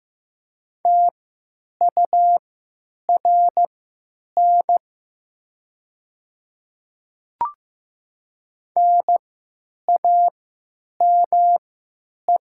Morse code, heard as a keyed tone, about 15 wpm, spaced as for 9 wpm, 700 Hz. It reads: TURN NAME
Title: 3x Character Spacing - Top 200 Words - 15wpm